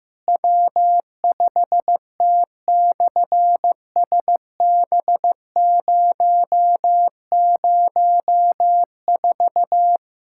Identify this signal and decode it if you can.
Morse code, heard as a keyed tone, about 15 wpm, 700 Hz. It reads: W5T/SB004
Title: SOTA QSO Practice 15wpm